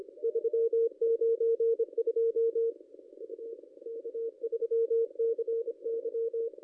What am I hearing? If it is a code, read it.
Morse code, heard as a keyed tone, about 26 wpm, 460 Hz, with radio noise. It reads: 392 U K3CY
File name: contests / ss-cw-k5tr-op-n5rz -0600